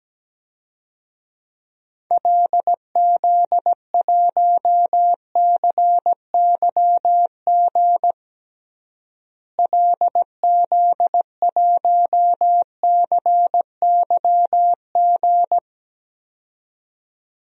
Morse code, heard as a keyed tone, about 17 wpm, 700 Hz. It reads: LZ1CYG LZ1CYG
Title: DXCC-100 Call Signs - 6 Characters - Repeated 2x - 17wpm